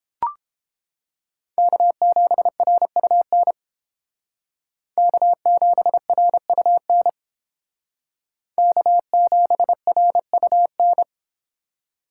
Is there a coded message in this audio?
Yes: Morse code from 1.6 to 11.0 s, heard as a keyed tone, about 30 wpm, 700 Hz. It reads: K7RUN K7RUN K7RUN